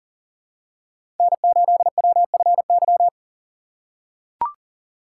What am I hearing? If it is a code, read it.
Morse code, heard as a keyed tone, about 40 wpm, 700 Hz. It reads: N8WFY
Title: USA Call Signs - 5 Characters - 4x Character Spacing - 40wpm